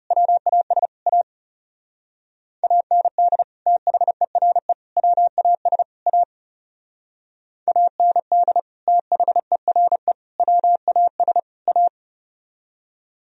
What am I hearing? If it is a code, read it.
Morse code, heard as a keyed tone, about 35 wpm, 700 Hz. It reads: WAS A AND THERE WAS A AND THERE WAS A
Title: Speed-Racing - Sets of 4 Words 30wpm